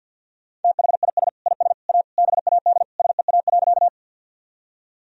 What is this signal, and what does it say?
Morse code, heard as a keyed tone, about 46 wpm, 700 Hz. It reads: THIS IS A BAD SEA.
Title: Sentences from Top 1000 Words - Review Part 1 - 50wpm